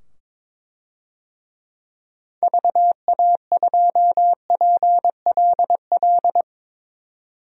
Morse code, heard as a keyed tone, about 22 wpm, 700 Hz. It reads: VA2PLL
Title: DXCC-100 Call Signs - 6 Characters - 22wpm